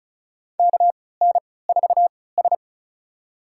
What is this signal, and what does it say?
Morse code, heard as a keyed tone, about 36 wpm, spaced as for 20 wpm, 700 Hz. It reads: KN4S
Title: USA Call Signs - 4 Characters - 3x Character Spacing - 35wpm